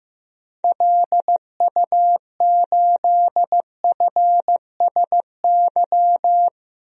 Morse code, heard as a keyed tone, about 15 wpm, 700 Hz. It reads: LU8FSY
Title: DXCC-100 Call Signs - 6 Characters - Repeated 2x - 15wpm